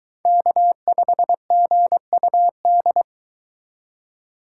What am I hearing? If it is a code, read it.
Morse code, heard as a keyed tone, about 22 wpm, 700 Hz. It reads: K5GUD